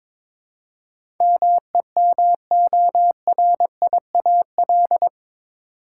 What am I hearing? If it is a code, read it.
Morse code, heard as a keyed tone, about 23 wpm, 700 Hz. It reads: MEMORIAL
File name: practice / 22wpm / Eight Letter Words - 22wpm